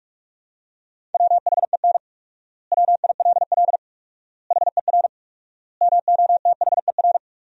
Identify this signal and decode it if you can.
Morse code, heard as a keyed tone, about 46 wpm, 700 Hz. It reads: WHEN WILL HER MOTHER
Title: 4x Word-Spacing - Sentences from Top 200 Words - 45wpm